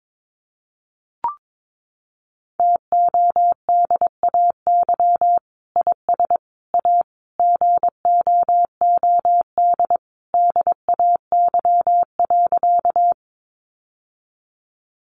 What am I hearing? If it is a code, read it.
Morse code, heard as a keyed tone, about 22 wpm, 700 Hz. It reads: TODAY IS A GOOD DAY.
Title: Sentences from Top 600 Words 22wpm